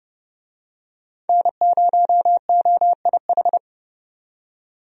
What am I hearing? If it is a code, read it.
Morse code, heard as a keyed tone, about 31 wpm, 700 Hz. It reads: N0OIH
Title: USA Call Signs - 5 Characters - 30wpm